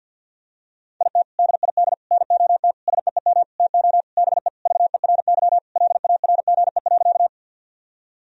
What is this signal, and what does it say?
Morse code, heard as a keyed tone, about 46 wpm, 700 Hz. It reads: IT DID NOT SEEM TO BE VERY LARGE.
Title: Sentences from Top 1000 Words - Review Part 1 - 50wpm